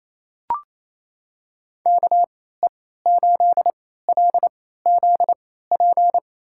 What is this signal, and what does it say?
Morse code, heard as a keyed tone, about 28 wpm, spaced as for 16 wpm, 700 Hz. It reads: KE8LZP
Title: USA Call Signs - 6 Characters - 3x Character Spacing - 28wpm